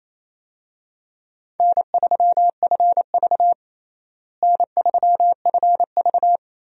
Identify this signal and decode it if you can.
Morse code, heard as a keyed tone, about 28 wpm, 700 Hz. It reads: N3FV N3FV
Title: USA Call Signs - 4 Characters - Repeated 3x - 28wpm